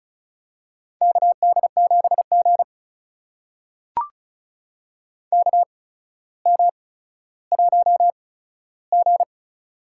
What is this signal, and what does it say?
Morse code, heard as a keyed tone, about 35 wpm, spaced as for 15 wpm, 700 Hz. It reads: KD7G KM1G
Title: USA Call Signs - 4 Characters - 8x Character Spacing - 35wpm